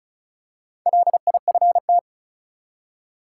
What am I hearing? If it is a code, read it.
Morse code, heard as a keyed tone, about 34 wpm, 700 Hz. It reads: LIFT
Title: Four Letter Words - 35wpm